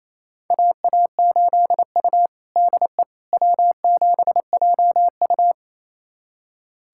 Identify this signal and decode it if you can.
Morse code, heard as a keyed tone, about 28 wpm, 700 Hz. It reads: AA8U DE W7JU
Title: USA Callsign DE Callsign 28wpm